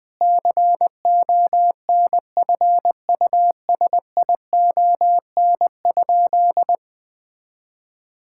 Morse code, heard as a keyed tone, about 21 wpm, 700 Hz. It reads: CONFUSION?
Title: Sentences from Top 1600 Words 20wpm